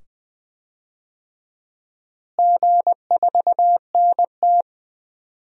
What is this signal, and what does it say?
Morse code, heard as a keyed tone, about 20 wpm, 700 Hz. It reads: G4NT
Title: DXCC-100 Call Signs - 4 Characters - Repeated 2x - 20wpm